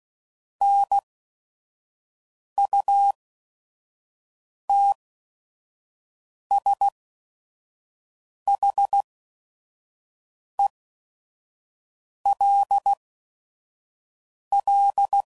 Morse code, heard as a keyed tone, about 16 wpm, spaced as for 5 wpm, 800 Hz. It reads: NUTSHELL
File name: Prefixes / By First Letter / Ns